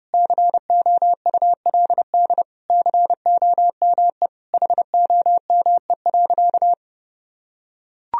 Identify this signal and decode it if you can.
Morse code, heard as a keyed tone, about 30 wpm, 700 Hz. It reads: COULD COME HOME.
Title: Sentences from Top 1000 Words - Review Part 3 - 30wpm